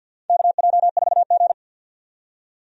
Morse code, heard as a keyed tone, about 52 wpm, 700 Hz. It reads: KJ4G
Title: USA Call Signs - 4 Characters - 50wpm